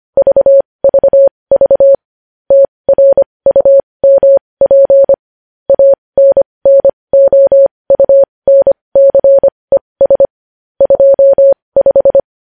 Morse code, heard as a keyed tone, about 25 wpm, spaced as for 20 wpm, 550 Hz. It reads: VVV TRUMP ANNOUNCES 25
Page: News Headlines in Morse Code at 20 WPM